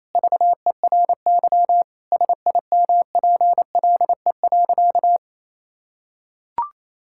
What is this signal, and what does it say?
Morse code, heard as a keyed tone, about 28 wpm, 700 Hz. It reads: VERY SIMPLE.
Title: Sentences from Top 1600 Words 28wpm